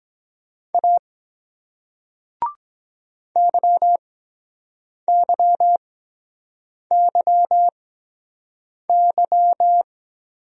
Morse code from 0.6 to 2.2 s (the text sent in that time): A